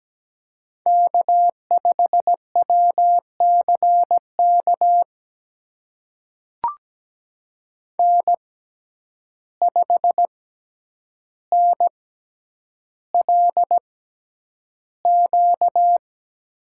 Morse code, heard as a keyed tone, about 17 wpm, spaced as for 9 wpm, 700 Hz. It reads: K5WCK N5NLQ
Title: USA Call Signs - 5 Characters - 6x Character Spacing - 17wpm